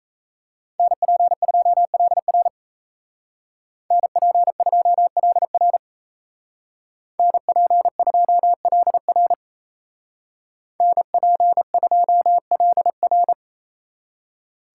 Morse code, heard as a keyed tone, about 33 wpm, 700 Hz. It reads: NP2LR NP2LR NP2LR NP2LR
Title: Speed Racing - USA 2x2 Call Signs 28wpm